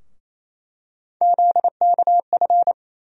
Morse code, heard as a keyed tone, about 28 wpm, 700 Hz. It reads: ZKF
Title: USA Call Sign Postfixes - 3 Letters 28wpm